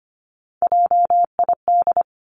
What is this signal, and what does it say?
Morse code, heard as a keyed tone, about 25 wpm, 700 Hz. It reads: JID
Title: Three Challenging Letter-Number Combinations 25wpm